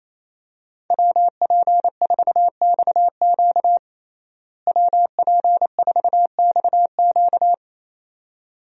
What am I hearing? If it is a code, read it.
Morse code, heard as a keyed tone, about 27 wpm, 700 Hz. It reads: WP4XQ WP4XQ
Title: USA Call Signs - 5 Characters - Repeated 2x - 28wpm